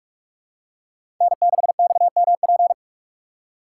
Morse code, heard as a keyed tone, about 43 wpm, 700 Hz. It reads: N6XMP
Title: USA Call Signs - 4x Character Spacing - 45wpm